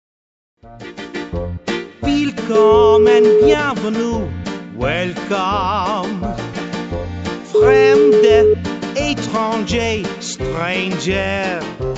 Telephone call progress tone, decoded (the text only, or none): ringback tone